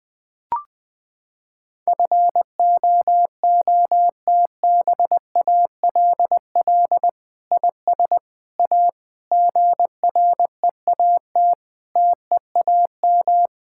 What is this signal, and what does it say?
Morse code, heard as a keyed tone, about 20 wpm, 700 Hz. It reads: FOOTBALL IS A GREAT TEAM
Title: Sentences from Top 1600 Words 20wpm